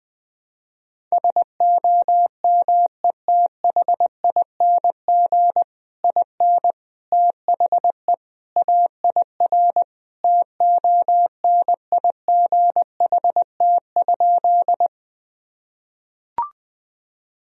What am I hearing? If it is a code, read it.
Morse code, heard as a keyed tone, about 20 wpm, 700 Hz. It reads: SOMETHING IN THE AIR TONIGHT?
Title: Sentences from Top 1000 Words - Review Part 1 - 20wpm